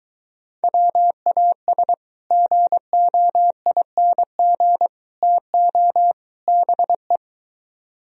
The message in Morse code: WAS GOING TO BE

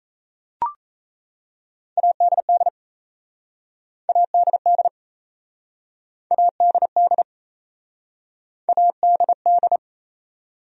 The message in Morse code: ADD ADD ADD ADD

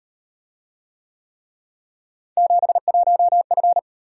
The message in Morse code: 71F

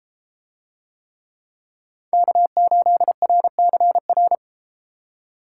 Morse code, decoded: K8RCR

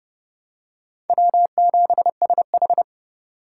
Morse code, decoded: W7SH

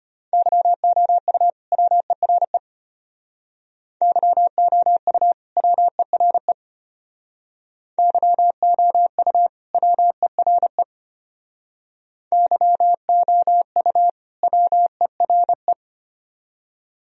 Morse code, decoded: YOU WERE YOU WERE YOU WERE YOU WERE

